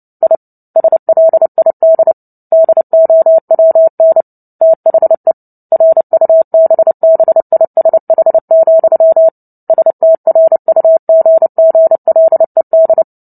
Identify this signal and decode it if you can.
Morse code, heard as a keyed tone, about 30 wpm, 650 Hz. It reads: I SLID DOWN THE RUBBISH, STRUGGLED
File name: Book Two / 03-Chapter 3 - 29 WPM